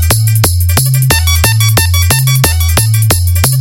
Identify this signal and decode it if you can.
There is no code.